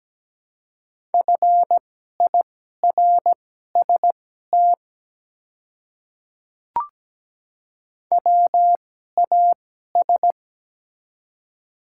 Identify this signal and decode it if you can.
Morse code, heard as a keyed tone, about 17 wpm, spaced as for 12 wpm, 700 Hz. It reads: FIRST WAS
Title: 2x Character Spacing - Top 100 Words - 17wpm